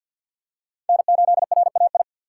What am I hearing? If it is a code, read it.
Morse code, heard as a keyed tone, about 48 wpm, 700 Hz. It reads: N8RAI
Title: USA Call Signs - 5 Characters - 4x Character Spacing - 50wpm